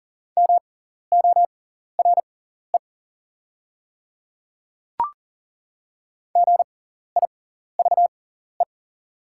Morse code, heard as a keyed tone, about 40 wpm, spaced as for 14 wpm, 690 Hz. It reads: MORE GIVE
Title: 6x Character Spacing - Top 300 Words - 40wpm